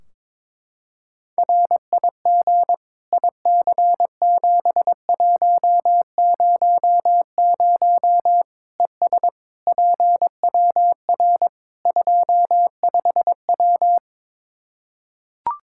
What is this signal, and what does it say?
Morse code, heard as a keyed tone, about 21 wpm, 700 Hz. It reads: RIG IC7100 ES PWR 25W
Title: K1USN SST QSO Practice 22wpm